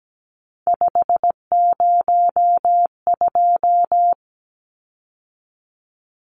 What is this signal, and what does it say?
Morse code, heard as a keyed tone, about 17 wpm, 700 Hz. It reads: 502
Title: Numbers - 3-Digit 17wpm